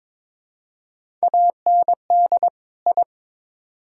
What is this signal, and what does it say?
Morse code, heard as a keyed tone, about 22 wpm, 700 Hz. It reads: AND I
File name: practice / 22wpm / Build A Phrase - 5-Word Phrases - 22wpm